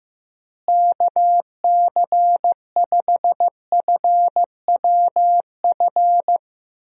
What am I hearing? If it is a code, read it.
Morse code, heard as a keyed tone, about 15 wpm, 700 Hz. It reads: KC5FWF
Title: USA Call Signs - 6 Characters - Repeated 2x - 15wpm